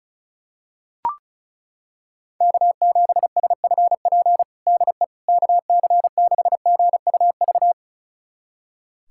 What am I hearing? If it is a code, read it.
Morse code, heard as a keyed tone, about 36 wpm, 700 Hz. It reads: K7SFP DE KC6GUV
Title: USA Callsign DE Callsign 35wpm